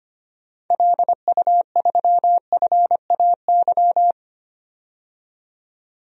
Morse code, heard as a keyed tone, about 25 wpm, 700 Hz. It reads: LU3FAY
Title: DXCC-100 Call Signs - 6 Characters - 25wpm